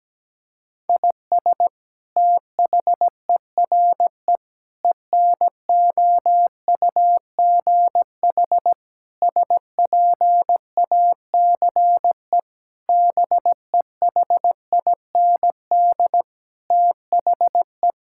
Morse code, heard as a keyed tone, about 17 wpm, 700 Hz. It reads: IS THERE ENOUGH SPACE BEHIND THE
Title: Sentences from Top 1000 Words - Review Part 5 - 17wpm